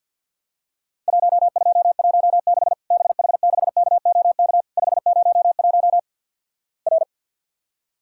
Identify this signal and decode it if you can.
Morse code, heard as a keyed tone, about 47 wpm, 700 Hz. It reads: 1216 BH6KOK 501 R
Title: Worked All Europe DX Contest - QTC Exchange - 50wpm